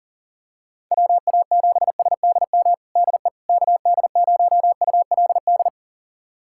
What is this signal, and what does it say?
Morse code, heard as a keyed tone, about 40 wpm, 700 Hz. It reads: WA7SDM DE KD0ULD
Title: USA Callsign DE Callsign - 6 Characters - 40wpm